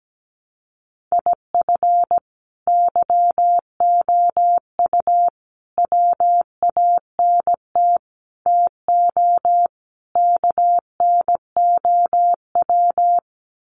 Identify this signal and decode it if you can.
Morse code, heard as a keyed tone, about 17 wpm, 700 Hz. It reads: IF YOU WANT TO KNOW
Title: Sets of 5 Words Encore2 17wpm